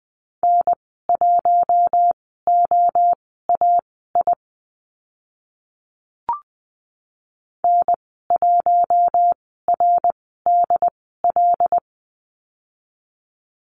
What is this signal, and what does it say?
Morse code, heard as a keyed tone, about 20 wpm, spaced as for 14 wpm, 700 Hz. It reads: N1OAI N1RDL
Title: 076 Lesson 13g - Letter L - Call Signs 20wpm